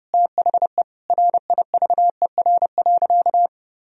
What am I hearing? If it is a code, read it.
Morse code, heard as a keyed tone, about 30 wpm, 700 Hz. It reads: THE RIVER.
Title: Sentences from Top 1000 Words - Review Part 4 - 30wpm